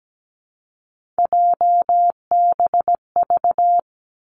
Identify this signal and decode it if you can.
Morse code, heard as a keyed tone, about 17 wpm, 700 Hz. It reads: JBV
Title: Three Challenging Letter-Number Combinations 17wpm